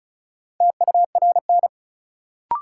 Morse code, heard as a keyed tone, about 36 wpm, 700 Hz. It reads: TURN
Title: Sentences from Top 200 Words - New Word Drill - 35wpm